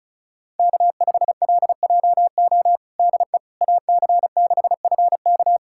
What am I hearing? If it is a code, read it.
Morse code, heard as a keyed tone, about 34 wpm, 700 Hz. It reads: K5LJO DE AC6FK
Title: USA Callsign DE Callsign - 5 Characters - 35wpm